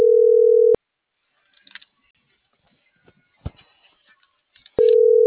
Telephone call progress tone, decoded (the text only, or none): ringback tone